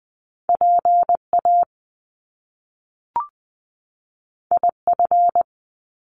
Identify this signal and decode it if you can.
Morse code, heard as a keyed tone, about 20 wpm, 700 Hz. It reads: PA IF
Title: Top Two Letter Combinations in Words 20wpm